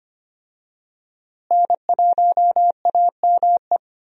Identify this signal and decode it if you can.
Morse code, heard as a keyed tone, about 25 wpm, 700 Hz. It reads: N1AME
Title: USA Call Signs - 5 Characters - Repeated 2x - 25wpm